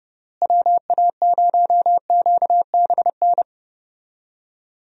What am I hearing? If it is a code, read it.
Morse code, heard as a keyed tone, about 30 wpm, 700 Hz. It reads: WA0QBN
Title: USA Call Signs - 6 Characters - Repeated 3x - 30wpm